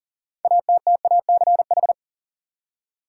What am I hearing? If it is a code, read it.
Morse code, heard as a keyed tone, about 37 wpm, 700 Hz. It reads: ATTACH
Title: Six Letter Words - 40wpm